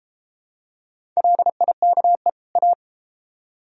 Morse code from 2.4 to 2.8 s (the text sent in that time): A